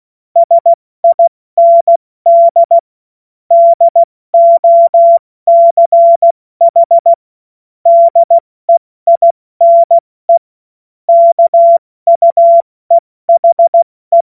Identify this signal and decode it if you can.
Morse code, heard as a keyed tone, about 16 wpm, 680 Hz. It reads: SIND DOCH DEINE KUEHE
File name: Morsekurs / Witze / Witz 13. April 2021, 70 F80